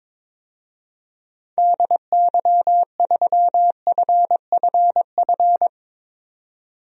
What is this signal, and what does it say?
Morse code, heard as a keyed tone, about 22 wpm, 700 Hz. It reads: DY3FFF
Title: DXCC-100 Call Signs - 6 Characters - 22wpm